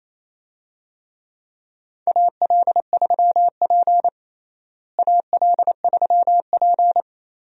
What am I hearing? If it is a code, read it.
Morse code, heard as a keyed tone, about 28 wpm, 700 Hz. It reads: AL3P AL3P
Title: USA Call Signs - 4 Characters - Repeated 3x - 28wpm